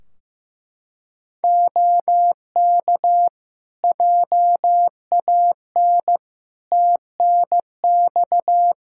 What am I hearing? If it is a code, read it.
Morse code, heard as a keyed tone, about 15 wpm, 700 Hz. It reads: OK JAN TNX
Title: K1USN SST QSO Practice 15wpm